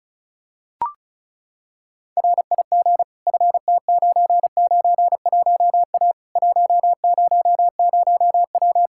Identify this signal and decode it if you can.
Morse code, heard as a keyed tone, about 36 wpm, 700 Hz. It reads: RIG FT991A 100W
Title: Practice QSO - Exchange 2 - Info - 35wpm